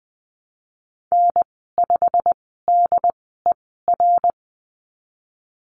Morse code, heard as a keyed tone, about 20 wpm, spaced as for 15 wpm, 700 Hz. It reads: N5DER
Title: 088 Lesson 15g - Number 5 - Call Signs 20wpm